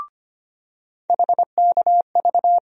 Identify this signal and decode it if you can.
Morse code, heard as a keyed tone, about 25 wpm, 700 Hz. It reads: HKV
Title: USA Call Sign Postfixes - 3 Letters 25wpm